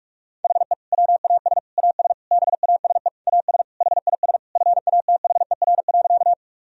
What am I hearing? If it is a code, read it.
Morse code, heard as a keyed tone, about 46 wpm, 700 Hz. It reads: HE WAS AS BASE AS HIS FATHER.
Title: Sentences from Top 1000 Words - Review Part 1 - 45wpm